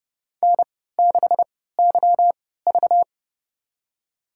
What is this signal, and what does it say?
Morse code, heard as a keyed tone, about 30 wpm, spaced as for 17 wpm, 700 Hz. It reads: N6YV
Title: USA Call Signs - 4 Characters - 3x Character Spacing - 30wpm